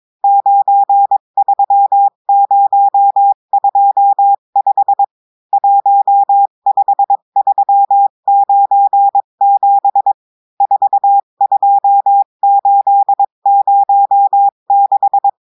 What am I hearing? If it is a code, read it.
Morse code, heard as a keyed tone, about 22 wpm, 800 Hz. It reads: 93025 15397 42806
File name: cours-du-mercredi / audios / cours-853-du-2-mai-2024-etxte chiffre et texte clair-a-20-wpm